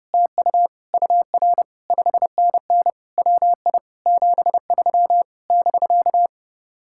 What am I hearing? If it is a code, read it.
Morse code, heard as a keyed tone, about 31 wpm, 700 Hz. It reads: TU UR 5NN WI 73 <BK>